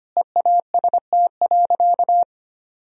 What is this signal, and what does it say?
Morse code, heard as a keyed tone, about 25 wpm, 700 Hz. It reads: EAST.
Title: Sentences from Top 1000 Words - Review Part 3 - 25wpm